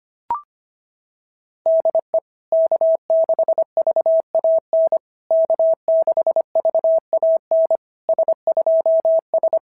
Morse code, heard as a keyed tone, about 25 wpm, 650 Hz. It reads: DE K6VAN K6VAN S2S
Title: SOTA QSO Practice 25wpm